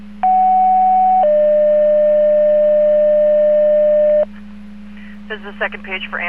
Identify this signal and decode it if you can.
Two-tone page, heard as a one-second tone, then a three-second tone, about 750 then 600 Hz, with speech. It is heard from 0.2 to 4.2 s.